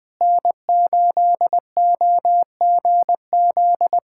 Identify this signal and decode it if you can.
Morse code, heard as a keyed tone, about 20 wpm, 700 Hz. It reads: N8OGZ
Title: USA Callsign DE Callsign - 5 Characters - 20wpm